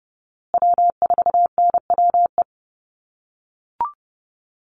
Morse code, heard as a keyed tone, about 30 wpm, 700 Hz. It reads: W4NWE
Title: USA Call Signs Repeated 2x 30wpm